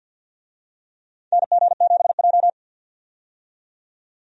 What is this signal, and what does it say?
Morse code, heard as a keyed tone, about 52 wpm, 700 Hz. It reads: NG7J
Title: USA Call Signs - 4 Characters - 50wpm